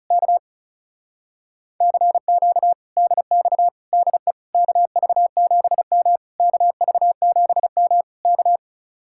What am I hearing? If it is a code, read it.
Morse code, heard as a keyed tone, about 33 wpm, 700 Hz. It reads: K CQ DX DE KV7M KV7M K
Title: Speed-Racing Common QSO Phrases - 30wpm